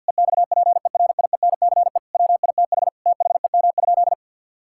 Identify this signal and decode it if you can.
Morse code, heard as a keyed tone, about 50 wpm, 700 Hz. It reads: EXPERIENCE WITH THEM?
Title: Sentences from Top 1000 Words - Review Part 5 - 50wpm